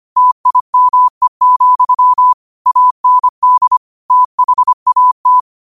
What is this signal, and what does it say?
Morse code, heard as a keyed tone, about 25 wpm, 1000 Hz. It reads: TIME, AND THAT
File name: qotdfiles / 25wpm